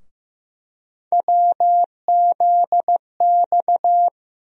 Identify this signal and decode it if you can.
Morse code, heard as a keyed tone, about 15 wpm, 700 Hz. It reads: WZX
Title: USA Call Sign Postfixes - 3 Letters 15wpm